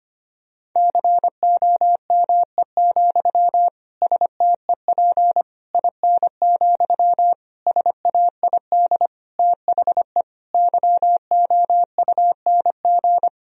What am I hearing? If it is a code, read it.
Morse code, heard as a keyed tone, about 25 wpm, 700 Hz. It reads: COME, STEP IN, SAID THE YOUNG
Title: Sentences from Top 1000 Words - Review Part 2 - 25wpm